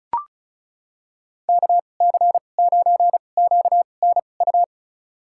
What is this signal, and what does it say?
Morse code, heard as a keyed tone, about 36 wpm, spaced as for 25 wpm, 700 Hz. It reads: KC9QNU
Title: USA Call Signs - 6 Characters - 2x Character Spacing - 35wpm